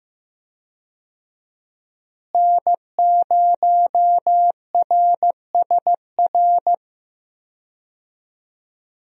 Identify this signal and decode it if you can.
Morse code, heard as a keyed tone, about 16 wpm, 700 Hz. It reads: N0RSR
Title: USA Call Signs - 5 Characters - Repeated 2x - 15wpm